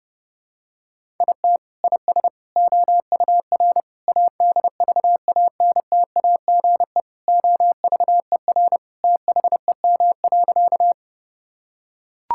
Morse code, heard as a keyed tone, about 30 wpm, 700 Hz. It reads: IT IS OUR ADVANTAGE OVER THEM.
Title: Sentences from Top 1000 Words - Review Part 6 - 30wpm